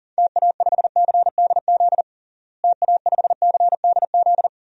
Morse code, heard as a keyed tone, about 40 wpm, 700 Hz. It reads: TA5CDZ TA5CDZ
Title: DXCC-100 Call Signs - 6 Characters - Repeated 2x - 40wpm